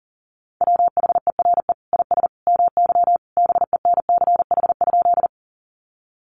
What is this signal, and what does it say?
Morse code, heard as a keyed tone, about 37 wpm, 700 Hz. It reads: WHERE IS MY BENCH?
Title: Sentences from Top 1400 Words 40wpm